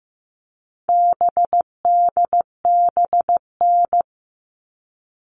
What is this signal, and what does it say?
Morse code, heard as a keyed tone, about 15 wpm, 700 Hz. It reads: BDBN
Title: All Cut Numbers - 4-Digit 15wpm